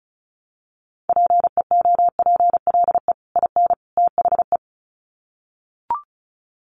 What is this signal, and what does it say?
Morse code, heard as a keyed tone, about 34 wpm, 700 Hz. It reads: PEOPLE IN THE
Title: Sets of 3 Words 35wpm